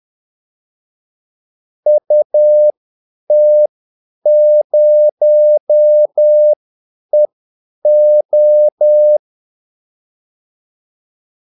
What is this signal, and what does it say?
Morse code, heard as a keyed tone, about 10 wpm, spaced as for 8 wpm, 600 Hz. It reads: UT0EO
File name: Callsigns-10-8wpm